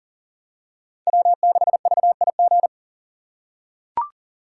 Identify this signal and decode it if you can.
Morse code, heard as a keyed tone, about 40 wpm, 700 Hz. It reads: W6VIG
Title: USA Call Signs - 5 Characters - 3x Character Spacing - 40wpm